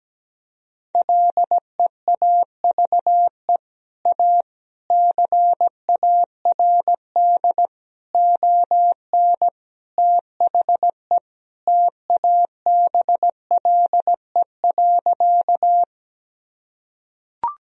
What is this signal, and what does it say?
Morse code, heard as a keyed tone, about 17 wpm, 700 Hz. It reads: LEAVE A CARD ON THE TABLE.
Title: Sentences from Top 1000 Words - Review Part 4 - 17wpm